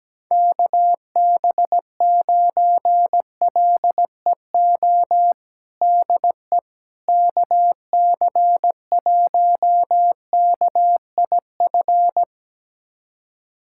Morse code, heard as a keyed tone, about 17 wpm, 700 Hz. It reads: KB9LEO DE KC1KIF
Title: USA Callsign DE Callsign - 6 Characters - 17wpm